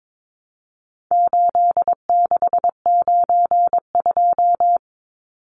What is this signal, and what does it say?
Morse code, heard as a keyed tone, about 22 wpm, 700 Hz. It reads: 8692
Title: Numbers - 4-Digit 22wpm